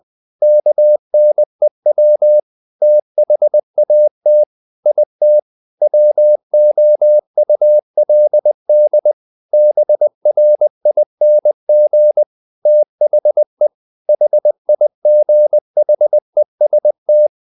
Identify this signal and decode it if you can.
Morse code, heard as a keyed tone, about 20 wpm, 600 Hz. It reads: KNEW THAT IT WOULD BRING THE HIGHEST
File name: the old man and the sea part 2 20wpm